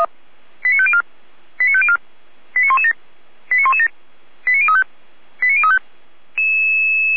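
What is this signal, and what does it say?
Five-tone selective call (ZVEI), heard as quick runs of five tones, tones of about 70 ms. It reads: 89504 89504 89117 89117 89035 89035
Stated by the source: ZVEI five-tone selective call 89094, received on an SDR.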